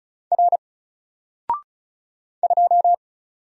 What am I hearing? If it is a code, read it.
Morse code, heard as a keyed tone, about 36 wpm, 700 Hz. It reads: R 2
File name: Single-Letter-Number-35wpm